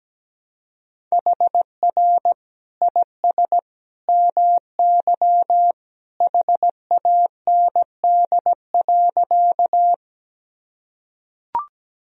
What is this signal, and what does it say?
Morse code, heard as a keyed tone, about 17 wpm, 700 Hz. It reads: HR IS MY HAND.
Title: Sentences from Top 200 Words and Abbreviations 17wpm